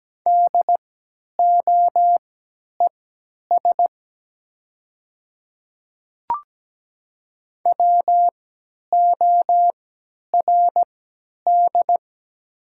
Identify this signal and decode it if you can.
Morse code, heard as a keyed tone, about 17 wpm, spaced as for 10 wpm, 700 Hz. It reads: DOES WORD
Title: 3x Character Spacing - Top 300 Words - 17wpm